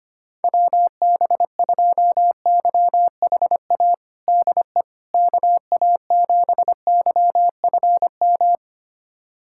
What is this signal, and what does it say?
Morse code, heard as a keyed tone, about 25 wpm, 700 Hz. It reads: WB2YHA DE KA7YFM
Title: USA Callsign DE Callsign - 6 Characters - 25wpm